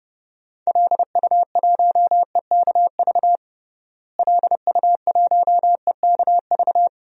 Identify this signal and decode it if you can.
Morse code, heard as a keyed tone, about 30 wpm, 700 Hz. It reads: LU1EKV LU1EKV
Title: DXCC-100 Call Signs - 6 Characters - Repeated 3x - 30wpm